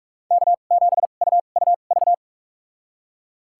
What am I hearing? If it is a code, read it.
Morse code, heard as a keyed tone, about 46 wpm, spaced as for 33 wpm, 700 Hz. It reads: K7UUV
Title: USA Call Signs - 2x Character Spacing - 45wpm